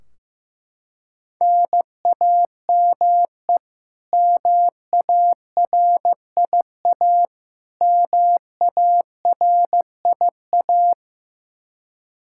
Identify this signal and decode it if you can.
Morse code, heard as a keyed tone, about 16 wpm, 700 Hz. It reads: NAME MARIA MARIA
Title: Practice QSO - Exchange 1 - Beginning - 15wpm